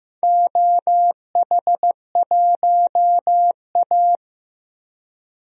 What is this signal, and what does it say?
Morse code, heard as a keyed tone, about 15 wpm, 700 Hz. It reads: OH1A
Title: DXCC-100 Call Signs - 4 Characters - Repeated 3x - 15wpm